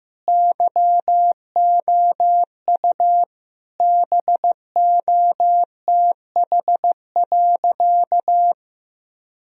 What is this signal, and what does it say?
Morse code, heard as a keyed tone, about 16 wpm, 700 Hz. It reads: YOU BOTH.
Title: Sentences from Top 1000 Words - Review Part 1 - 15wpm